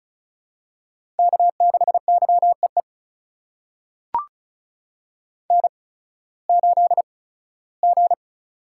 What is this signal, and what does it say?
Morse code, heard as a keyed tone, about 36 wpm, spaced as for 19 wpm, 700 Hz. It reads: K6YEE N8G
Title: USA Call Signs - 5 Characters - 8x Character Spacing - 35wpm